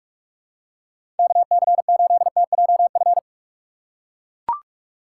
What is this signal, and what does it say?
Morse code, heard as a keyed tone, about 42 wpm, 700 Hz. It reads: KC8TJF
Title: USA Call Signs - 6 Characters - 2x Character Spacing - 45wpm